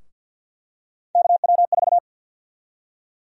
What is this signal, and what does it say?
Morse code, heard as a keyed tone, about 51 wpm, 700 Hz. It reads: KW4
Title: USA Call Sign Prefixes - Repeated 2x - 50wpm